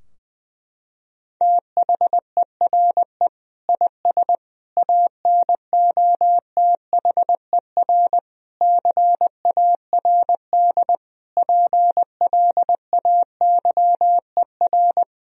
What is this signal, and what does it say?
Morse code, heard as a keyed tone, about 20 wpm, 700 Hz. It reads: THERE IS ANOTHER CARD PLAYER
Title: Sentences from Top 1000 Words - Review Part 4 - 20wpm